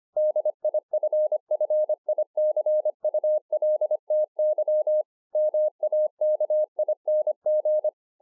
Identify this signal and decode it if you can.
Morse code, heard as a keyed tone, about 25 wpm, 610 Hz. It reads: DIFFICULTY MAKING
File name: 2011-04-mp3 / THIS IS NOT MY SHACK 2 25WPM